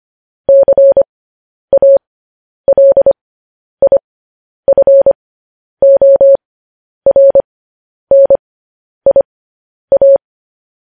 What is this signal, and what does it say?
Morse code, heard as a keyed tone, about 25 wpm, spaced as for 10 wpm, 550 Hz. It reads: CALIFORNIA
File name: podcast / News headlines-1x-25x10